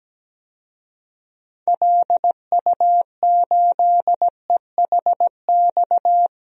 Morse code, heard as a keyed tone, about 17 wpm, 700 Hz. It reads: LU8EHX
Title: DXCC-100 Call Signs - 6 Characters - Repeated 3x - 17wpm